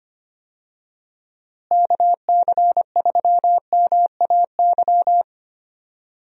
Morse code, heard as a keyed tone, about 25 wpm, 700 Hz. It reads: KC3MAY
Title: USA Call Signs - 6 Characters - Repeated 3x - 25wpm